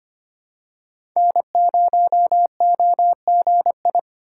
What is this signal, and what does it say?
Morse code, heard as a keyed tone, about 25 wpm, 700 Hz. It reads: N0OGI
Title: USA Call Signs - 5 Characters - 4x Character Spacing - 25wpm